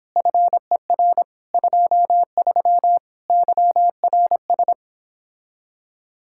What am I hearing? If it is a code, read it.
Morse code, heard as a keyed tone, about 31 wpm, 700 Hz. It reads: FER 23 YRS